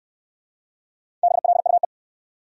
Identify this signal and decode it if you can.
Morse code, heard as a keyed tone, about 63 wpm, 700 Hz. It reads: BLUE